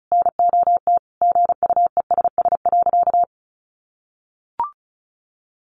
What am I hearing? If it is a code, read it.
Morse code, heard as a keyed tone, about 34 wpm, 700 Hz. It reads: NOT GUESS.
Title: Sentences from Top 1500 Words 35wpm